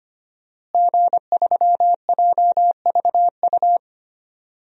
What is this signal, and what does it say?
Morse code, heard as a keyed tone, about 25 wpm, 700 Hz. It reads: G3JVU
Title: DXCC-100 Call Signs - 5 Characters - Repeated 2x - 25wpm